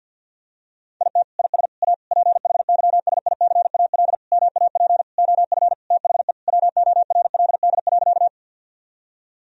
Morse code, heard as a keyed tone, about 50 wpm, 700 Hz. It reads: IT IS A PHYSICAL MAP OF THE WORLD.
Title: Sentences from Top 1000 Words - Review Part 4 - 50wpm